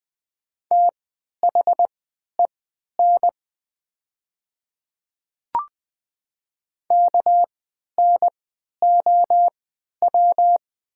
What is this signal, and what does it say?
Morse code, heard as a keyed tone, about 20 wpm, spaced as for 11 wpm, 700 Hz. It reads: THEN KNOW
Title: 3x Character Spacing - Top 100 Words - 20wpm